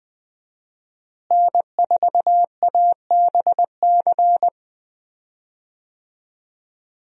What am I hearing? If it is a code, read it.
Morse code, heard as a keyed tone, about 20 wpm, 700 Hz. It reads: N4ABC
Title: USA Call Signs - 5 Characters - Repeated 3x - 20wpm